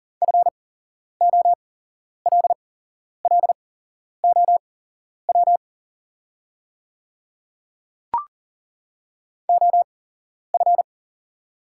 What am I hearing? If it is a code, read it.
Morse code, heard as a keyed tone, about 39 wpm, spaced as for 11 wpm, 690 Hz. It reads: FOLLOW OF
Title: 8x Character Spacing - Top 500 Words - 40wpm